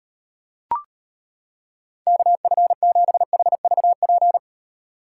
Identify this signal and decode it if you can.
Morse code, heard as a keyed tone, about 38 wpm, 700 Hz. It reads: KF7HVP